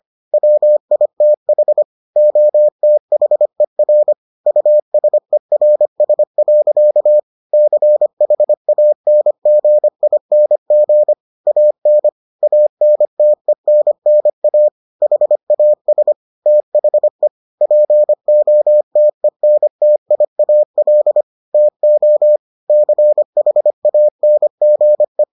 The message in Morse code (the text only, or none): WITH OTHER USERS. CHANGING AN ANTENNA HAS THE POTENTIAL TO CHANGE